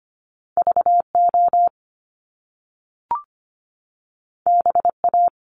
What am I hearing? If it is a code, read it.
Morse code, heard as a keyed tone, about 25 wpm, 700 Hz. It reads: VO BA
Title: Top Two Letter Combinations in Words 25wpm